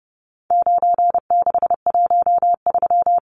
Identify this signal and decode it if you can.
Morse code, heard as a keyed tone, about 30 wpm, 700 Hz. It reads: 9613
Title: Numbers - 4-Digit 30wpm